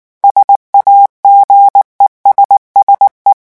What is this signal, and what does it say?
Morse code, heard as a keyed tone, about 19 wpm, 800 Hz. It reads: SAGESSE
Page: REVISION 8 - 8 mots minute